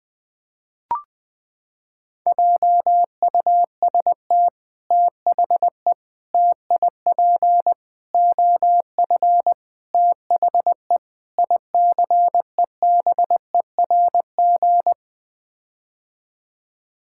Morse code, heard as a keyed tone, about 20 wpm, 700 Hz. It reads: JUST THE TIP OF THE ICEBERG
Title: Sets of 6 Words - 20wpm